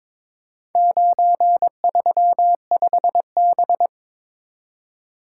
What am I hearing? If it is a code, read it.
Morse code, heard as a keyed tone, about 22 wpm, 700 Hz. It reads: 935B
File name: practice / 22wpm / Four Letter-Number Combinations - 22wpm